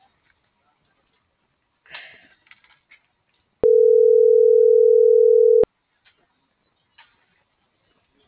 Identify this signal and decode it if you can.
Ringback tone (North America), 440 + 480 Hz.